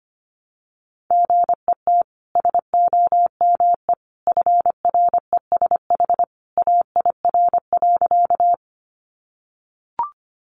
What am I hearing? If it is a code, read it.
Morse code, heard as a keyed tone, about 25 wpm, 700 Hz. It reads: GET SOME FRESH AIR.
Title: Sentences from Top 1200 Words 25wpm